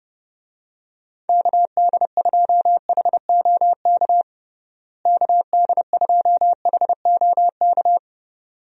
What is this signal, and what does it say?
Morse code, heard as a keyed tone, about 30 wpm, 700 Hz. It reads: KD2HOK KD2HOK
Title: USA Call Signs - 6 Characters - Repeated 2x - 30wpm